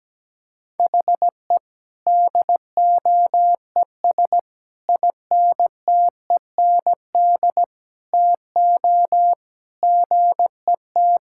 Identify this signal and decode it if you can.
Morse code, heard as a keyed tone, about 17 wpm, 700 Hz. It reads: HE DOES INTEND TO GET
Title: Sentences from Top 1600 Words 17wpm